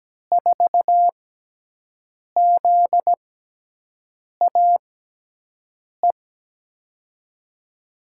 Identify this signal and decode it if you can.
Morse code, heard as a keyed tone, about 17 wpm, spaced as for 6 wpm, 700 Hz. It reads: 4ZAE
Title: USA Call Signs - 6 Characters - 6x Character Spacing - 17wpm